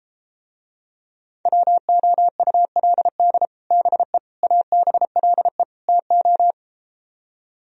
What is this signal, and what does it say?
Morse code, heard as a keyed tone, about 33 wpm, 700 Hz. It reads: WOULD BE ABLE TO